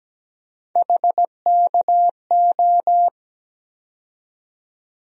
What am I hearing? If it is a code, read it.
Morse code, heard as a keyed tone, about 17 wpm, 700 Hz. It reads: HKO
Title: USA Call Sign Postfixes - 3 Letters 17wpm